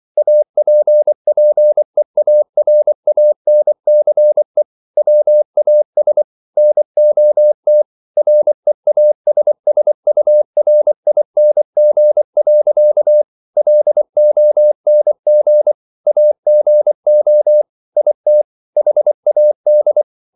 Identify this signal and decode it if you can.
Morse code, heard as a keyed tone, about 24 wpm, 600 Hz. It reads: APPEARANCE WAS NOT REASSURING. LONG AGO IT HAD